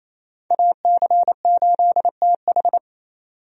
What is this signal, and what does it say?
Morse code, heard as a keyed tone, about 28 wpm, 700 Hz. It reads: AC8TH
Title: USA Call Signs - 5 Characters - 28wpm